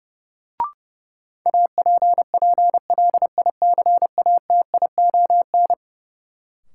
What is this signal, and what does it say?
Morse code, heard as a keyed tone, about 31 wpm, 700 Hz. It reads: APPLICATION
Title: Sentences from Top 1200 Words - New Word Drill Alt - 30wpm